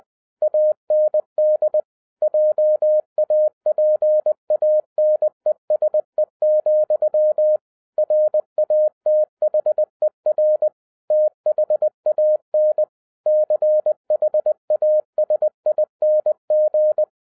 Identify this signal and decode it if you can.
Morse code, heard as a keyed tone, about 20 wpm, 610 Hz. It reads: AND JAPANESE, RATHER THAN CHASING
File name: 2012-01-mp3 / n05-CAME BACK TO HAM RADIO AFTER 30 YEARS OF INTERVAL-20wpm